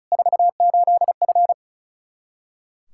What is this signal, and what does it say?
Morse code, heard as a keyed tone, about 36 wpm, 700 Hz. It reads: 48F
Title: Practice QSO - Exchange 2 - Info - 35wpm